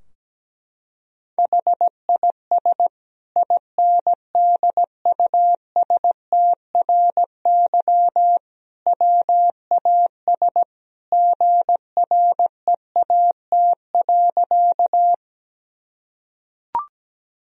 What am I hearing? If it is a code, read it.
Morse code, heard as a keyed tone, about 17 wpm, 700 Hz. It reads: HIS INDUSTRY WAS GREAT.
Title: Sentences from Top 1000 Words - Review Part 4 - 17wpm